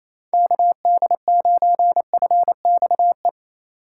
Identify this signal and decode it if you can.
Morse code, heard as a keyed tone, about 28 wpm, 690 Hz. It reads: KD9FXE
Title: USA Call Signs - 6 Characters - 4x Character Spacing - 28wpm